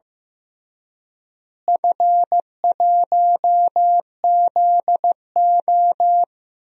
Morse code, heard as a keyed tone, about 15 wpm, 700 Hz. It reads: F1ZO
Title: DXCC-100 Call Signs - 4 Characters - Repeated 2x - 15wpm